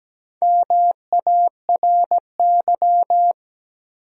Morse code, heard as a keyed tone, about 17 wpm, 700 Hz. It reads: MARY
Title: USA Common Names - 17wpm